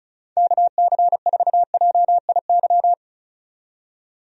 Morse code, heard as a keyed tone, about 34 wpm, 700 Hz. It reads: KC4JIY
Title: USA Call Signs - 6 Characters - 35wpm